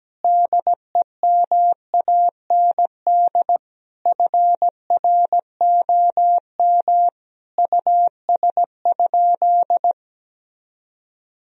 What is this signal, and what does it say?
Morse code, heard as a keyed tone, about 17 wpm, 700 Hz. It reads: DEMAND FROM US?
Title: Sentences from Top 1000 Words - Review Part 4 - 17wpm